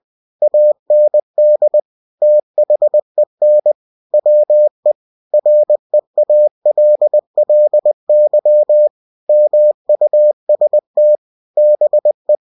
Morse code, heard as a keyed tone, about 20 wpm, 600 Hz. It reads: AND THEN WE REALLY MUST BE
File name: cw-books / winnie-the-pooh / winnie-20wpm-page-0068